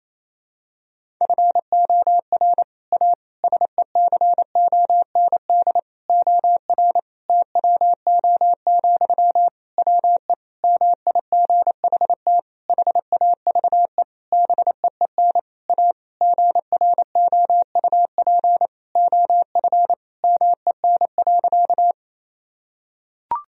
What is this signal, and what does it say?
Morse code, heard as a keyed tone, about 28 wpm, 700 Hz. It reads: FOR A SECOND OR TWO, WE MIGHT HAVE BEEN A GROUP OF MEN.
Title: Sentences from Top 1000 Words - Review Part 1 - 28wpm